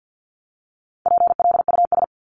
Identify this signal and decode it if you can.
Morse code, heard as a keyed tone, about 50 wpm, 700 Hz. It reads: PLUS
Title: Top 1001-1250 Words 50wpm